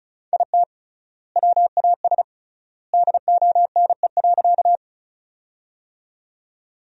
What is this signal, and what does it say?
Morse code, heard as a keyed tone, about 34 wpm, 700 Hz. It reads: IT WAS DONE.
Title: 3x Word-Spacing - Sentences from Top 500 Words - 35wpm